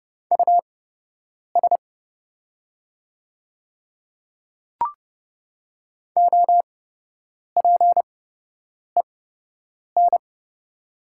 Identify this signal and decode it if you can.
Morse code, heard as a keyed tone, about 30 wpm, spaced as for 8 wpm, 700 Hz. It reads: US OPEN
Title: 8x Character Spacing - Top 300 Words - 30wpm